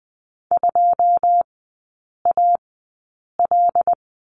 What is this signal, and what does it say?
Morse code, heard as a keyed tone, about 20 wpm, spaced as for 8 wpm, 700 Hz. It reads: 2AL
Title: 114 Lesson 19e - Letter w - Three Characters 20wpm